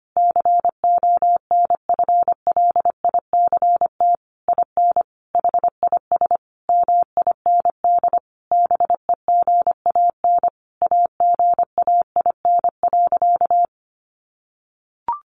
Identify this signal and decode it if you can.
Morse code, heard as a keyed tone, about 25 wpm, 700 Hz. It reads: CONFLICT IN HIS MIND BEGAN AGAIN.
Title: Sentences from Top 1300 Words 25wpm